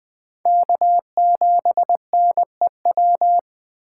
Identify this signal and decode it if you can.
Morse code, heard as a keyed tone, about 20 wpm, 700 Hz. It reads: K7NEW
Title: 2023 SOTA Chaser Call Signs - 20wpm